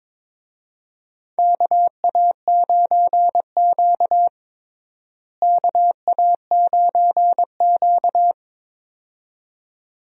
Morse code, heard as a keyed tone, about 22 wpm, 700 Hz. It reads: KA9Q KA9Q
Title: USA Call Signs - 4 Characters - Repeated 2x - 22wpm